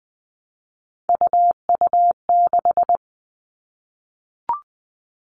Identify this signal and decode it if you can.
Morse code, heard as a keyed tone, about 20 wpm, 700 Hz. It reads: UU6
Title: All Cut Numbers - 3-Digit 20wpm